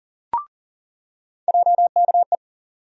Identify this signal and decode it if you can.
Morse code, heard as a keyed tone, about 40 wpm, 700 Hz. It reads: JKE
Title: USA Call Sign Postfixes - 3 Letters 40wpm